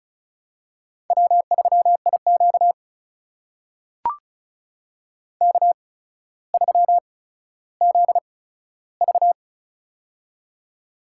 Morse code, heard as a keyed tone, about 36 wpm, spaced as for 15 wpm, 700 Hz. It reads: W3IQ K3ZV
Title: USA Call Signs - 4 Characters - 8x Character Spacing - 35wpm